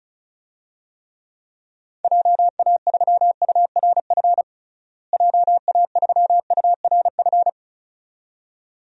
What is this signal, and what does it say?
Morse code, heard as a keyed tone, about 36 wpm, 700 Hz. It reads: JA3URF JA3URF
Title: DXCC-100 Call Signs - 6 Characters - Repeated 2x - 35wpm